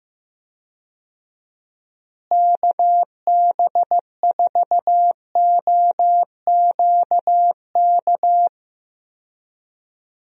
Morse code, heard as a keyed tone, about 15 wpm, 700 Hz. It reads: KB4OQK
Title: USA Call Signs - 6 Characters - Repeated 3x - 15wpm